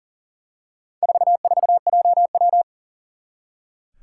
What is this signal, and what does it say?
Morse code, heard as a keyed tone, about 40 wpm, 700 Hz. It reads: 44JW
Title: Four Challenging Letter-Number Combinations 40wpm